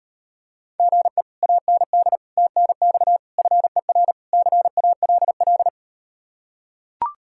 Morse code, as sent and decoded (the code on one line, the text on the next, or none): --. / .- -. -.. / - -. -..- / ..-. .-. / -.-. .- .-.. .-..
GE AND TNX FER CALL